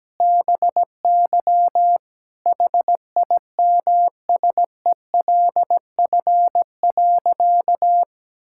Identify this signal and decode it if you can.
Morse code, heard as a keyed tone, about 17 wpm, 700 Hz. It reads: BY HIMSELF.
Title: Sentences from Top 1600 Words 17wpm